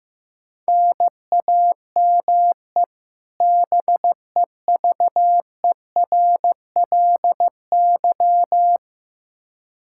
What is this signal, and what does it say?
Morse code, heard as a keyed tone, about 15 wpm, 700 Hz. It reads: NAME BEVERLY
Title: Practice QSO - Exchange 1 - Beginning - 15wpm